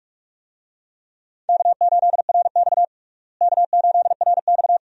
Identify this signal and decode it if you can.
Morse code, heard as a keyed tone, about 43 wpm, 700 Hz. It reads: K8RX K8RX
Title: USA Call Signs - 4 Characters - Repeated 2x - 45wpm